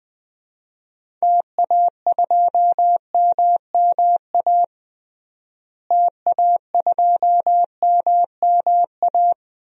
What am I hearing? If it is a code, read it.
Morse code, heard as a keyed tone, about 20 wpm, 700 Hz. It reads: TA2MMA TA2MMA
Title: DXCC-100 Call Signs - 6 Characters - Repeated 2x - 20wpm